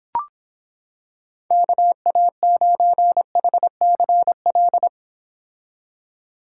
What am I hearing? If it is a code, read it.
Morse code, heard as a keyed tone, about 26 wpm, 700 Hz. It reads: KA9HCL